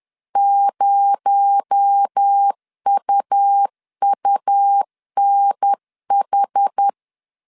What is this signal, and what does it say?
Morse code, heard as a keyed tone, about 10 wpm, 800 Hz. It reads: 0UUNH
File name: CW / DET / 1 6 1500C S C 10wpm 4252